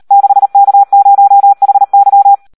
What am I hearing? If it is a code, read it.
Morse code, heard as a keyed tone, about 38 wpm, 800 Hz. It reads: 6K0HY
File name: Audio practica CW / SD 1 38